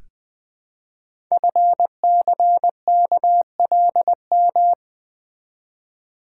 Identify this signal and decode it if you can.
Morse code, heard as a keyed tone, about 20 wpm, 700 Hz. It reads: FCKLM
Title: Code Groups - 5 Letters - 20wpm